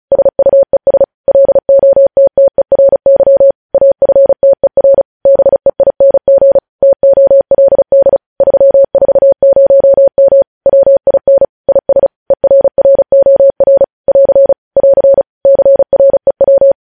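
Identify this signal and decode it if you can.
Morse code, heard as a keyed tone, about 36 wpm, 550 Hz. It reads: SUES LOTTERY AFTER BEING TOLD 340M WIN IS ERROR <AR> <AR> CREW
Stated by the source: Morse code news headlines sent at 35 wpm.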